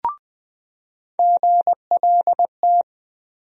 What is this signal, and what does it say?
Morse code, heard as a keyed tone, about 20 wpm, 700 Hz. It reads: GLT
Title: USA Call Sign Postfixes - 3 Letters 20wpm